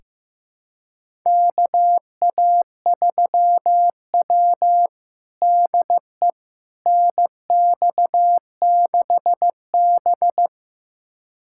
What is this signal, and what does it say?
Morse code, heard as a keyed tone, about 15 wpm, 700 Hz. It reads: KA3W DE NX6B
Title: USA Callsign DE Callsign - 4 Characters - 15wpm